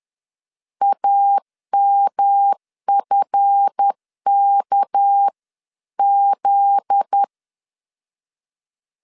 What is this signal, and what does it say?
Morse code, heard as a keyed tone, about 10 wpm, 800 Hz. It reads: AMFKZ
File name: LEZIONI / 7 GF9Z / 7 GF9Z 10wpm 2611